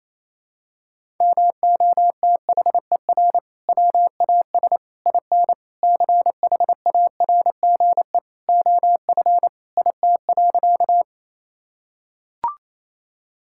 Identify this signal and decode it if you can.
Morse code, heard as a keyed tone, about 28 wpm, 700 Hz. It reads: MOTHER WAS IN CHARGE OF IT.
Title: Sentences from Top 1000 Words - Review Part 6 - 28wpm